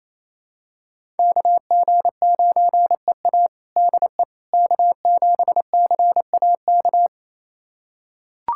Morse code, heard as a keyed tone, about 28 wpm, 700 Hz. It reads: KG9EA DE K7CAK
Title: USA Callsign DE Callsign - 5 Characters - 28wpm